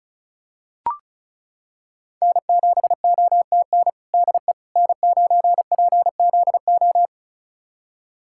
Morse code, heard as a keyed tone, about 36 wpm, 700 Hz. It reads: N7OTN DE N9PZO
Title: USA Callsign DE Callsign - 5 Characters - 35wpm